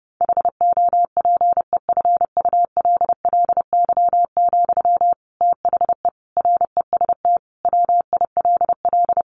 Morse code, heard as a keyed tone, about 30 wpm, 700 Hz. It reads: HOPEFULLY, THE REST WILL
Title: Sentences from Top 1500 Words 30wpm